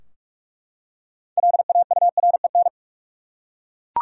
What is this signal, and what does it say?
Morse code, heard as a keyed tone, about 42 wpm, 700 Hz. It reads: LAUREN
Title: USA Common Names - 45wpm